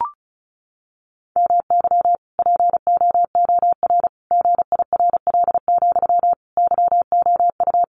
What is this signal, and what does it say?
Morse code, heard as a keyed tone, about 34 wpm, 700 Hz. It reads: MY POOR GIRL, YOU
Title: Sentences from Top 1000 Words 35wpm